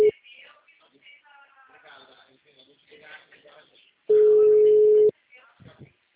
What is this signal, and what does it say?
Ringback tone (Europe, 425 Hz), with speech.
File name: out- -2053- - - .4774